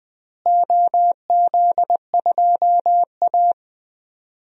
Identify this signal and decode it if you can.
Morse code, heard as a keyed tone, about 20 wpm, 700 Hz. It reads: OZ2A
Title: DXCC-100 Call Signs - 4 Characters - Repeated 3x - 20wpm